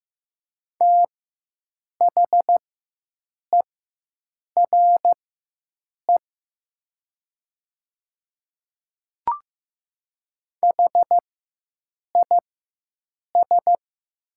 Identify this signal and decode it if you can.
Morse code, heard as a keyed tone, about 15 wpm, spaced as for 7 wpm, 700 Hz. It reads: THERE HIS
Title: 4x Character Spacing - Top 100 Words - 15wpm